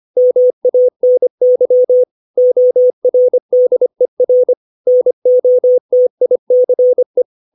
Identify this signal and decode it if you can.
Morse code, heard as a keyed tone, about 25 wpm, 500 Hz. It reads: MANY ORDER NOTICE
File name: 632 words / 632 Words 25WPM